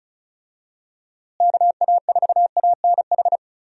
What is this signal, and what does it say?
Morse code, heard as a keyed tone, about 34 wpm, 700 Hz. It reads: KA4ANH
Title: USA Call Signs - 6 Characters - Repeated 2x - 35wpm